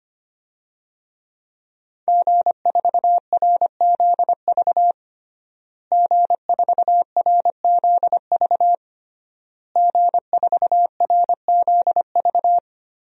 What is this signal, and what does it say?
Morse code, heard as a keyed tone, about 25 wpm, 700 Hz. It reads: G4RZV G4RZV G4RZV
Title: DXCC-100 Call Signs - 5 Characters - Repeated 3x - 25wpm